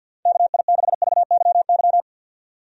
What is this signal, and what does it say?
Morse code, heard as a keyed tone, about 52 wpm, 700 Hz. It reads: KI6VYY
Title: USA Call Signs - 6 Characters - Repeated 3x - 50wpm